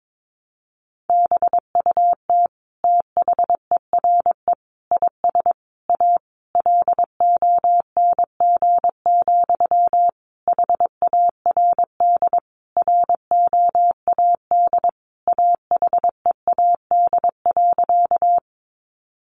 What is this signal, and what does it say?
Morse code, heard as a keyed tone, about 22 wpm, 700 Hz. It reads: BUT THERE IS A LONG, HARD ROAD AHEAD.
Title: Sentences from Top 1100 Words 22wpm